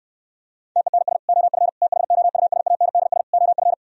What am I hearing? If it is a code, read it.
Morse code, heard as a keyed tone, about 69 wpm, 700 Hz. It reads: TENS OF THOUSANDS OF